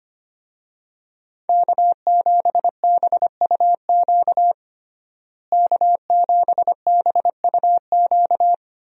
Morse code, heard as a keyed tone, about 25 wpm, 700 Hz. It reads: K7BUQ K7BUQ
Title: USA Call Signs - 5 Characters - Repeated 2x - 25wpm